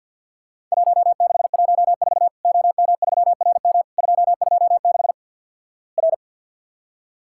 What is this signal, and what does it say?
Morse code, heard as a keyed tone, about 50 wpm, 700 Hz. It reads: 1614 OM3RM 226 R
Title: Worked All Europe DX Contest - QTC Exchange - 50wpm